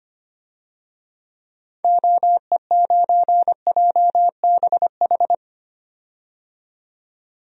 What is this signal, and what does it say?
Morse code, heard as a keyed tone, about 25 wpm, 700 Hz. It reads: OE9JBH
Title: DXCC-100 Call Signs - 6 Characters - 25wpm